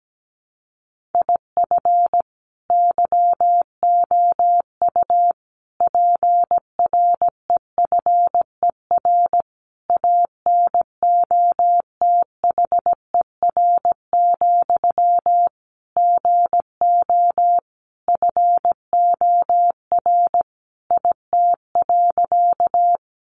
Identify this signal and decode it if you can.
Morse code, heard as a keyed tone, about 17 wpm, 700 Hz. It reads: IF YOU PREFER ANOTHER, GO FOR IT.
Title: Sentences from Top 1100 Words 17wpm